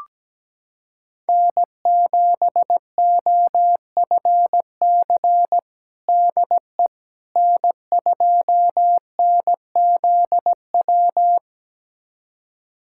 Morse code, heard as a keyed tone, about 17 wpm, 700 Hz. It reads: N7OFC DE N2NZW
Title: USA Callsign DE Callsign - 5 Characters - 17wpm